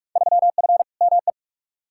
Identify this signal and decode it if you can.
Morse code, heard as a keyed tone, about 46 wpm, 700 Hz. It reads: 3F ME
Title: Field Day Exchanges - 45wpm